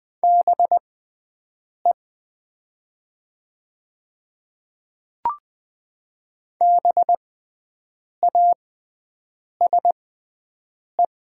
Morse code, heard as a keyed tone, about 20 wpm, spaced as for 7 wpm, 700 Hz. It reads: BE BASE
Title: 6x Character Spacing - Top 300 Words - 20wpm